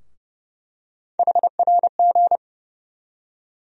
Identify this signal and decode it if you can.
Morse code, heard as a keyed tone, about 30 wpm, 700 Hz. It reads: HRG
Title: USA Call Sign Postfixes - 3 Letters 30wpm